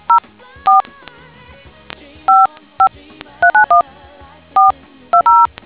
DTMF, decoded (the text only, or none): 04 55391420